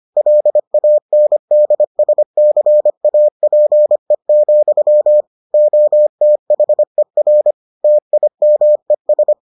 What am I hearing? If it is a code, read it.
Morse code, heard as a keyed tone, about 25 wpm, 600 Hz. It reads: LANDSCAPE, OTHER TIMES